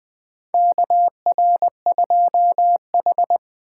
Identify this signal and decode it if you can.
Morse code, heard as a keyed tone, about 20 wpm, 700 Hz. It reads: KR2H
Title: USA Call Signs - 4 Characters - 8x Character Spacing - 20wpm